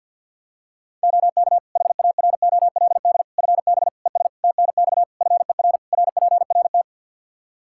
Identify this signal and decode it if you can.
Morse code, heard as a keyed tone, about 47 wpm, 700 Hz. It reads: OK HAROLD FB ES TNX FER RPRT
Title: Practice QSO - Exchange 2 - Info - 50wpm